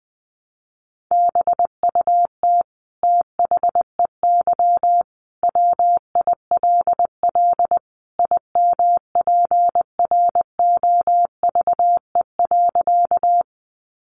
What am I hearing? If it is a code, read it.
Morse code, heard as a keyed tone, about 20 wpm, 700 Hz. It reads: BUT THEY WILL IMPROVE.
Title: Sentences from Top 800 Words 20wpm